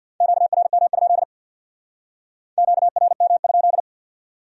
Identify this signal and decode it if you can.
Morse code, heard as a keyed tone, about 62 wpm, 700 Hz. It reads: QRM? QRM?
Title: Speed-Racing Q-Codes - 50wpm